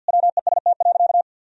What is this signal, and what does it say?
Morse code, heard as a keyed tone, about 52 wpm, 700 Hz. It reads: WEST.
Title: Sentences from Top 1000 Words - Review Part 3 - 50wpm